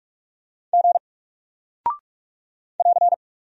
Morse code, heard as a keyed tone, about 44 wpm, 700 Hz. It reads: G <AR>
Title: Single Character (Letter:Number:Punctuation:Prosign) 45wpm